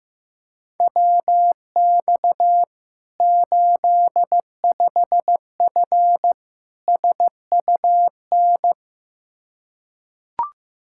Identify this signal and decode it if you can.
Morse code, heard as a keyed tone, about 15 wpm, 700 Hz. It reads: WX 85F SUN
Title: Practice QSO - Exchange 2 - Info - 15wpm